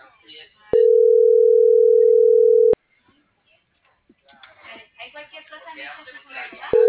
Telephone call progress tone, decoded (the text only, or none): ringback tone